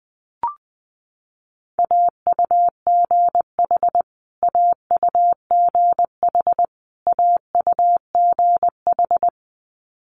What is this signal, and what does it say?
Morse code, heard as a keyed tone, about 20 wpm, 700 Hz. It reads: AUGH AUGH AUGH
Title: Multi-letter Phonemes Repeated 3x 20wpm